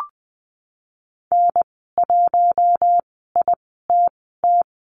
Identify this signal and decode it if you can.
Morse code, heard as a keyed tone, about 20 wpm, spaced as for 15 wpm, 700 Hz. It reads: N1ITT
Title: 042 Lesson 8g - Number 1 - Call Signs 20wpm